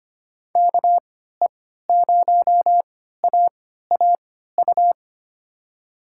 Morse code, heard as a keyed tone, about 25 wpm, spaced as for 14 wpm, 700 Hz. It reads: KE0AAU
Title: USA Call Signs - 3x Character Spacing - 25wpm